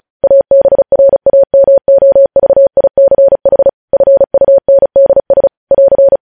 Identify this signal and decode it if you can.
Morse code, heard as a keyed tone, about 36 wpm, 550 Hz. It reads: ABRAMOVICH FUNDS <AR>
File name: podcast / News headlines-1x-35x35